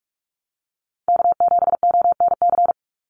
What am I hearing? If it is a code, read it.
Morse code, heard as a keyed tone, about 44 wpm, 700 Hz. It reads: K7ONC
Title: USA Call Signs 45wpm